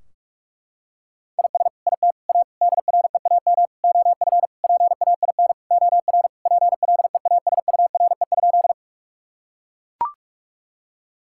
Morse code, heard as a keyed tone, about 42 wpm, 700 Hz. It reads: IS IT A DREAM OF PAIN OR PLEASURE?
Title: Sentences from Top 1600 Words 45wpm